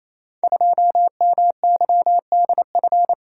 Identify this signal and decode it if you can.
Morse code, heard as a keyed tone, about 28 wpm, 700 Hz. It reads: 2MYDF
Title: Code Groups - 5 Letter:Numbers - 28wpm